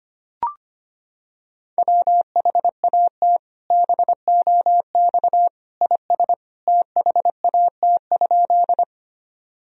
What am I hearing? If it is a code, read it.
Morse code, heard as a keyed tone, about 25 wpm, 700 Hz. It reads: WHAT BOX IS THAT?
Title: Sentences from Top 1000 Words - Review Part 2 - 25wpm